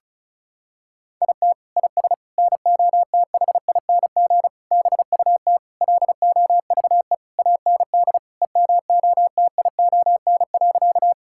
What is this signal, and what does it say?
Morse code, heard as a keyed tone, about 36 wpm, 700 Hz. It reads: IT IS NOTHING BUT LOVE AND EMOTION.
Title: Sentences from Top 1600 Words 35wpm